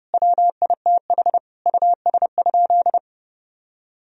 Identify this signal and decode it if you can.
Morse code, heard as a keyed tone, about 32 wpm, 700 Hz. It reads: WITH US?
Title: Sentences from Top 1000 Words - Review Part 3 - 30wpm